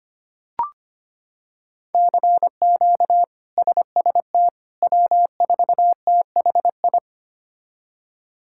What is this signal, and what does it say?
Morse code, heard as a keyed tone, about 25 wpm, 700 Hz. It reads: CQ SST W4THI
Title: K1USN SST QSO Practice - 25wpm